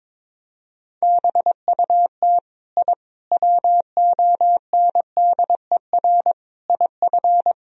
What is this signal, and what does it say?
Morse code, heard as a keyed tone, about 22 wpm, 700 Hz. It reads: BUT I WONDER IF
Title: Sentences from Top 1000 Words - Review Part 3 - 22wpm